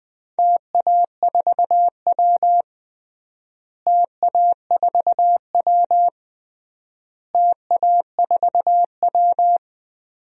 Morse code, heard as a keyed tone, about 20 wpm, 700 Hz. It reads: TA4W TA4W TA4W
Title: DXCC-100 Call Signs - 4 Characters - Repeated 3x - 20wpm